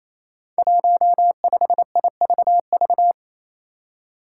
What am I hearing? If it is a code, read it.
Morse code, heard as a keyed tone, about 28 wpm, 700 Hz. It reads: 15IVV
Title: Five Challenging Letter-Number Combinations 28wpm